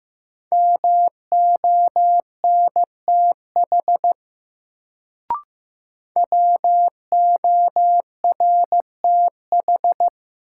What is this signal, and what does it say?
Morse code, heard as a keyed tone, about 15 wpm, 700 Hz. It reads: MONTH WORTH
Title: Sentences from Top 800 Words - New Word Drill - 15wpm